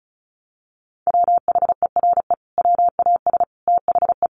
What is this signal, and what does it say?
Morse code, heard as a keyed tone, about 34 wpm, 700 Hz. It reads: WHERE WAS THE
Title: Sentences from Top 1000 Words 35wpm